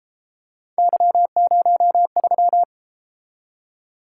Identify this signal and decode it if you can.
Morse code, heard as a keyed tone, about 32 wpm, 700 Hz. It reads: Y03